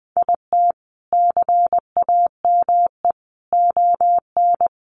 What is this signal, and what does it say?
Morse code, heard as a keyed tone, about 20 wpm, 700 Hz. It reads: IT CAME ON
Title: A Sound Of Thunder 20wpm